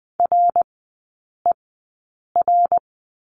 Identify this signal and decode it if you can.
Morse code, heard as a keyed tone, about 20 wpm, spaced as for 8 wpm, 700 Hz. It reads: RER
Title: 074 Lesson 13e - Letter L - Three Characters 20wpm